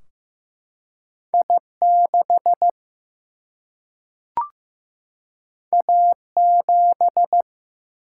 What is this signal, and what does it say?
Morse code, heard as a keyed tone, about 15 wpm, 700 Hz. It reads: I6 A7
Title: Single-Letter Single-Number 15wpm